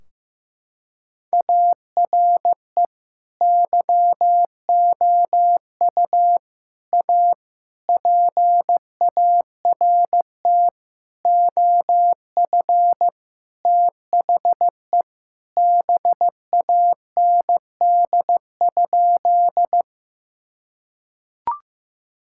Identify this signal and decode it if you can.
Morse code, heard as a keyed tone, about 15 wpm, 700 Hz. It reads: ARE YOU A PART OF THE BAND?
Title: Sentences from Top 1600 Words 15wpm